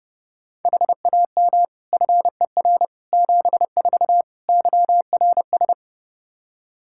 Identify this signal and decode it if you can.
Morse code, heard as a keyed tone, about 29 wpm, 700 Hz. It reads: HAM FER 74 YRS
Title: Practice QSO - Exchange 3 - More Info - 30wpm